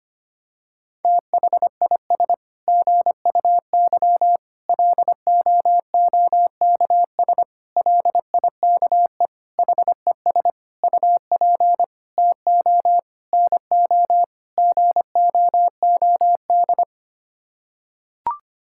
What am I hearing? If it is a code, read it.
Morse code, heard as a keyed tone, about 25 wpm, 700 Hz. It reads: THIS GUY LOOKS LIKE HES UP TO NO GOOD
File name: Sets of 9 Words - 25wpm